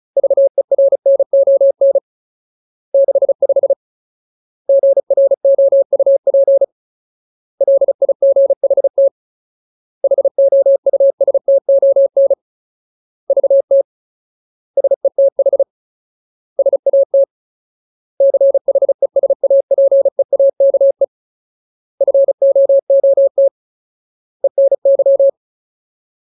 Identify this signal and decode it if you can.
Morse code, heard as a keyed tone, about 34 wpm, 550 Hz. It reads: VERNON 65 GROUP LIGHT HOUSTON VT SETH SAT CHESAPEAKE FOOT ENY
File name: Ultimate Word List / 35WPM Ultimate